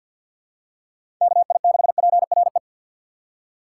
Morse code, heard as a keyed tone, about 51 wpm, 700 Hz. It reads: KI6PRE
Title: USA Call Signs - 6 Characters - 6x Character Spacing - 50wpm